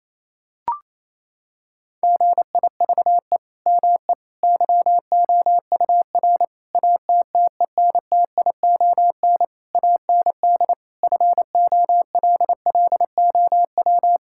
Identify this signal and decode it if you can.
Morse code, heard as a keyed tone, about 27 wpm, 700 Hz. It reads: GIVE ME YOUR ATTENTION AND FOLLOW
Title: Sentences from Top 1000 Words - Review Part 6 - 28wpm